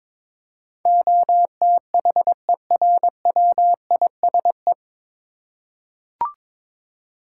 Morse code, heard as a keyed tone, about 22 wpm, 700 Hz. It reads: OTHERWISE
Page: Morse Code Ninja practice 22wpm